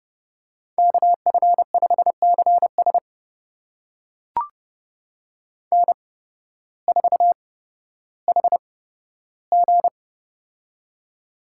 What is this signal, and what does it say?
Morse code, heard as a keyed tone, about 30 wpm, spaced as for 14 wpm, 700 Hz. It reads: KF5CS N4HG
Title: USA Call Signs - 8x Character Spacing - 30wpm